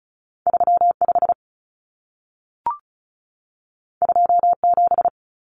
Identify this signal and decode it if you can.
Morse code, heard as a keyed tone, about 36 wpm, 700 Hz. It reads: 35 27
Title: Numbers - 2-Digit 35wpm